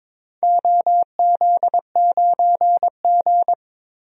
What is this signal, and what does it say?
Morse code, heard as a keyed tone, about 22 wpm, 700 Hz. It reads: OZ9G
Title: DXCC-100 Call Signs - 4 Characters - 22wpm